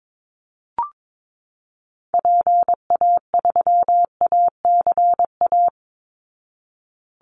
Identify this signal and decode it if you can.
Morse code, heard as a keyed tone, about 22 wpm, 700 Hz. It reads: PA3ACA
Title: Call Signs Repeated 3x - 22wpm